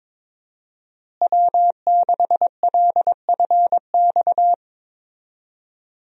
Morse code, heard as a keyed tone, about 22 wpm, 700 Hz. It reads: W6LFX
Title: USA Call Signs - 5 Characters - Repeated 2x - 22wpm